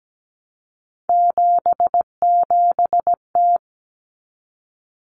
Morse code, heard as a keyed tone, about 17 wpm, 700 Hz. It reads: 77T
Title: Common Cut Numbers - 3-Digit 17wpm